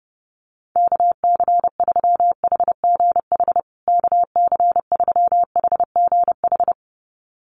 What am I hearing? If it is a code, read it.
Morse code, heard as a keyed tone, about 30 wpm, 700 Hz. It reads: KC3HGH KC3HGH
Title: USA Call Signs Repeated 2x 30wpm